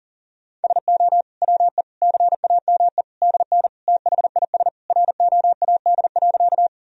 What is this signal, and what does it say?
Morse code, heard as a keyed tone, about 40 wpm, 700 Hz. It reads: SO WE CAME DN THIS ROAD.
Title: Sentences from Top 400 Words and Abbreviations 40wpm